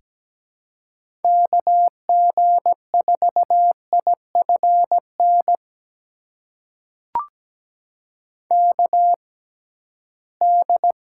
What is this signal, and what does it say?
Morse code, heard as a keyed tone, about 17 wpm, spaced as for 13 wpm, 700 Hz. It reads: KG4IFN KD
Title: USA Call Signs - 6 Characters - 6x Character Spacing - 17wpm